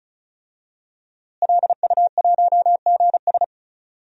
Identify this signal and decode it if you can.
Morse code, heard as a keyed tone, about 34 wpm, 700 Hz. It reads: LU1GS
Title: DXCC-100 Call Signs - 5 Characters - Repeated 3x - 35wpm